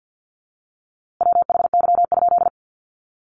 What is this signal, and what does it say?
Morse code, heard as a keyed tone, about 49 wpm, 700 Hz. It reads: WHY?
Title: A Sound Of Thunder 50wpm